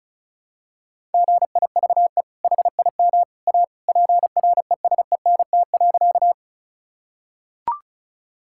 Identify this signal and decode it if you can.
Morse code, heard as a keyed tone, about 34 wpm, 700 Hz. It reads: GIVE HIM A PRESENT.
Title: Sentences from Top 1000 Words - Review Part 5 - 35wpm